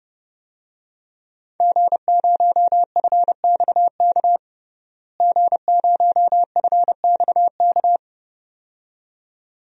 Morse code, heard as a keyed tone, about 32 wpm, 700 Hz. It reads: G0FXK G0FXK
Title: DXCC-100 Call Signs - 5 Characters - Repeated 2x - 30wpm